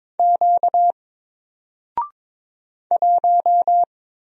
Morse code, heard as a keyed tone, about 22 wpm, 700 Hz. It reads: Q 1
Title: Single Character (Letter:Number:Punctuation:Prosign) Alt 22wpm